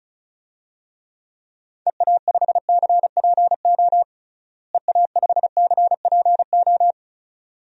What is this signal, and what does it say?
Morse code, heard as a keyed tone, about 36 wpm, 700 Hz. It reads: EA5CPO EA5CPO
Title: DXCC-100 Call Signs - 6 Characters - Repeated 2x - 35wpm